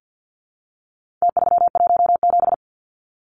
Morse code, heard as a keyed tone, about 50 wpm, 700 Hz. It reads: T317
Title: Numbers - Contest Style - 4-Digit 50wpm